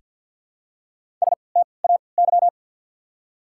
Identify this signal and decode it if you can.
Morse code, heard as a keyed tone, about 49 wpm, spaced as for 28 wpm, 700 Hz. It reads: STAY
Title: 3x Character Spacing - Top 500 Words - 50wpm